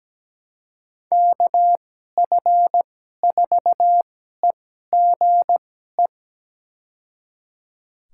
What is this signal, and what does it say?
Morse code, heard as a keyed tone, about 17 wpm, spaced as for 12 wpm, 700 Hz. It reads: KF4EGE
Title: USA Call Signs - 6 Characters - 2x Character Spacing - 17wpm